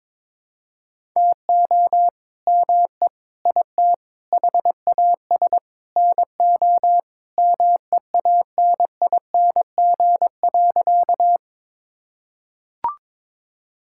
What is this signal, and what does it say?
Morse code, heard as a keyed tone, about 22 wpm, 700 Hz. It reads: TO ME IT HAS NO MEANING.
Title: Sentences from Top 1000 Words - Review Part 6 - 22wpm